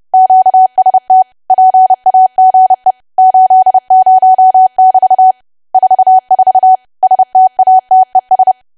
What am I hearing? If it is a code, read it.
Morse code, heard as a keyed tone, about 30 wpm, 750 Hz. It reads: QST PAGE 80<BT> 44 STATES
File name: cwa / W1AW-Level3 / 30WPM